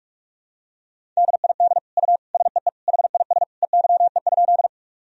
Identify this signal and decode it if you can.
Morse code, heard as a keyed tone, about 45 wpm, 700 Hz. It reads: DID U SEE HIS EYE?
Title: Sentences from Top 300 Words and Abbreviations 45wpm